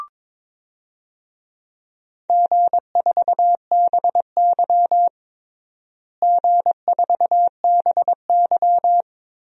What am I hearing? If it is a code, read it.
Morse code, heard as a keyed tone, about 21 wpm, 700 Hz. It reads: G4BY G4BY
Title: DXCC-100 Call Signs - 4 Characters - Repeated 3x - 22wpm